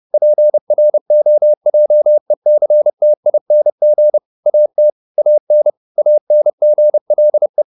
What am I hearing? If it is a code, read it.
Morse code, heard as a keyed tone, about 30 wpm, 600 Hz. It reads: PROJECTING AT AN ANGLE